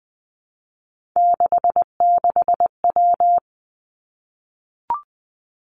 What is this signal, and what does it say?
Morse code, heard as a keyed tone, about 20 wpm, 700 Hz. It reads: 66W
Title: Three Challenging Letter-Number Combinations 20wpm